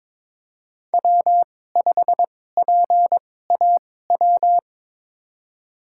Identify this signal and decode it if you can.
Morse code, heard as a keyed tone, about 22 wpm, spaced as for 16 wpm, 700 Hz. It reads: W5PAW
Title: USA Call Signs - 5 Characters - 2x Character Spacing - 22wpm